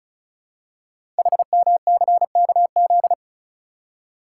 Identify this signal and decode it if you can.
Morse code, heard as a keyed tone, about 36 wpm, 690 Hz. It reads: HMCKZ
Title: Code Groups - 5 Letters - 35wpm